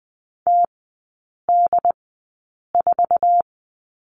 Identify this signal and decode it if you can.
Morse code, heard as a keyed tone, about 20 wpm, spaced as for 8 wpm, 700 Hz. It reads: TD4
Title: 067 Lesson 12e - Letter d - Three Characters 20wpm